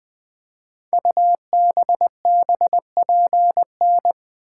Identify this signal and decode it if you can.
Morse code, heard as a keyed tone, about 20 wpm, 700 Hz. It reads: UBBPN
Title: Code Groups - 5 Letter:Numbers - 20wpm